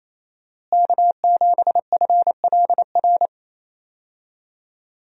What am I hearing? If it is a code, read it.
Morse code, heard as a keyed tone, about 28 wpm, 700 Hz. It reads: K7FLR
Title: USA Call Signs - 5 Characters - Repeated 3x - 28wpm